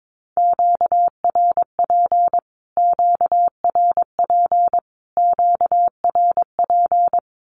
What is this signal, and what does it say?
Morse code, heard as a keyed tone, about 22 wpm, 700 Hz. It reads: QRP QRP QRP
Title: Q-Codes Repeated 3x 22wpm